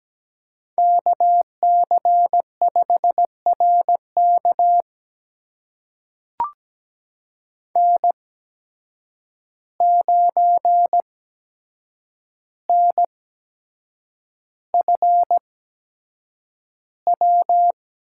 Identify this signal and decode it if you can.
Morse code, heard as a keyed tone, about 17 wpm, spaced as for 7 wpm, 700 Hz. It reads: KC5RK N9NFW
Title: USA Call Signs - 5 Characters - 8x Character Spacing - 17wpm